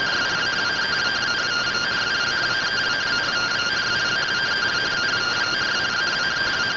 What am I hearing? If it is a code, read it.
Two-tone teleprinter signal, about 170 Hz shift.